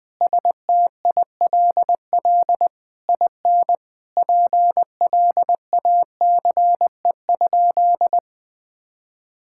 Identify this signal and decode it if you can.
Morse code, heard as a keyed tone, about 21 wpm, 700 Hz. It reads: STILL IN PLACE?
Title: Sentences from Top 1000 Words - Review Part 6 - 20wpm